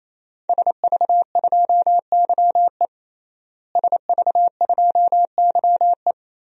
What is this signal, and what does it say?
Morse code, heard as a keyed tone, about 28 wpm, 700 Hz. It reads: SV2YE SV2YE
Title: DXCC-100 Call Signs - 5 Characters - Repeated 3x - 28wpm